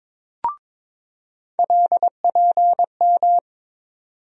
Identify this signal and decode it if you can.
Morse code, heard as a keyed tone, about 21 wpm, 700 Hz. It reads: LPM
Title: USA Call Sign Postfixes - 3 Letters 22wpm